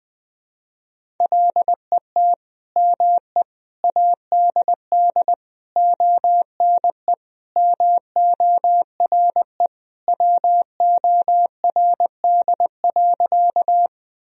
Morse code, heard as a keyed tone, about 21 wpm, 700 Hz. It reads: LET ME ADD ONE MORE WORD.
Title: Sentences from Top 1000 Words - Review Part 1 - 20wpm